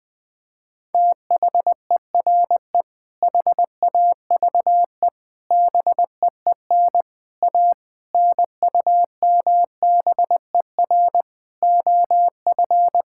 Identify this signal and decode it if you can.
Morse code, heard as a keyed tone, about 20 wpm, 700 Hz. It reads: THERE HAVE BEEN A NUMBER OF
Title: Sentences with 6-Word Phrases - 20wpm